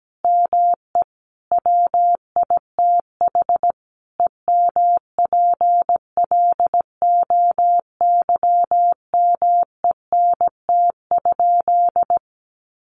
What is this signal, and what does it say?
Morse code, heard as a keyed tone, about 17 wpm, 700 Hz. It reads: ME WITH EMPLOYMENT?
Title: Sentences from Top 1500 Words 17wpm